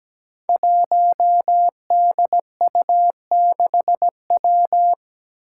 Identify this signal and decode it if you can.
Morse code, heard as a keyed tone, about 17 wpm, 700 Hz. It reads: 1DU6W
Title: Five Challenging Letter-Number Combinations 17wpm